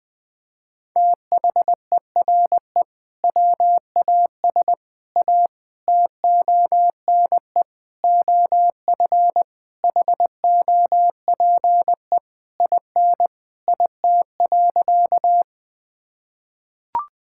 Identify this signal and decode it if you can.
Morse code, heard as a keyed tone, about 20 wpm, 700 Hz. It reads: THERE WAS A TONE OF HOPE IN IT.
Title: Sentences from Top 1600 Words 20wpm